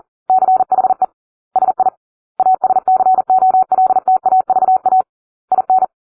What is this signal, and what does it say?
Morse code, heard as a keyed tone, about 40 wpm, 750 Hz. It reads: CHE SI ASCOLTAVA IN